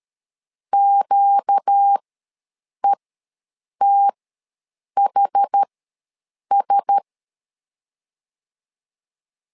Morse code, heard as a keyed tone, about 13 wpm, spaced as for 7 wpm, 800 Hz. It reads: QETHS
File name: LEZIONI / 4 Q1S ws 12wpm 1827